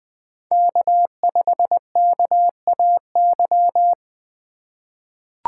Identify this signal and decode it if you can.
Morse code, heard as a keyed tone, about 20 wpm, 700 Hz. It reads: K5KAY
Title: USA Call Signs - 5 Characters - 2x Character Spacing - 20wpm